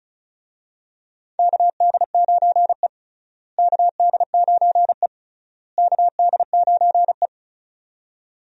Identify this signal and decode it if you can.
Morse code, heard as a keyed tone, about 36 wpm, 700 Hz. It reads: KD9E KD9E KD9E
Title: USA Call Signs - 4 Characters - Repeated 3x - 35wpm